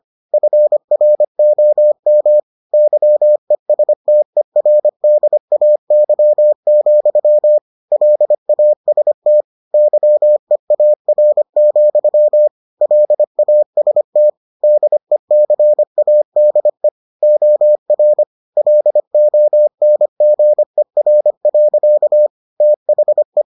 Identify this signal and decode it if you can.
Morse code, heard as a keyed tone, about 25 wpm, 600 Hz. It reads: FROM YESTERDAY, LAST YEAR, LAST DECADE OR LONGER. THE